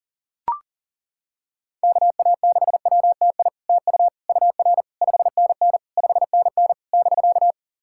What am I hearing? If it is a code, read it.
Morse code, heard as a keyed tone, about 38 wpm, 690 Hz. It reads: KA6WTI TU UR 5NN 5NN <BK>
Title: SOTA QSO Practice 40wpm